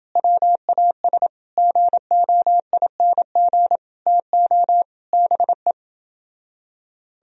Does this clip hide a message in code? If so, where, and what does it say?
Yes: Morse code from 0.2 to 5.7 s, heard as a keyed tone, about 28 wpm, 700 Hz. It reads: WAS GOING TO BE